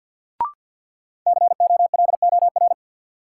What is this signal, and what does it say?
Morse code, heard as a keyed tone, about 53 wpm, 700 Hz. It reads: COLOR
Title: Sentences from Top 1000 Words - Review Part 2 - New Word Drill - 50wpm